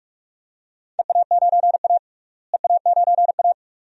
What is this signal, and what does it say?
Morse code, heard as a keyed tone, about 43 wpm, 700 Hz. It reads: EA9A EA9A
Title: DXCC-100 Call Signs - 4 Characters - Repeated 2x - 45wpm